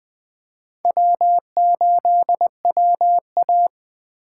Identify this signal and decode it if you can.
Morse code, heard as a keyed tone, about 22 wpm, 700 Hz. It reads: W8WA